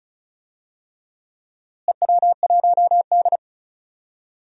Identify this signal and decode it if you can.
Morse code, heard as a keyed tone, about 34 wpm, 700 Hz. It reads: EW1D
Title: DXCC-100 Call Signs - 4 Characters - Repeated 3x - 35wpm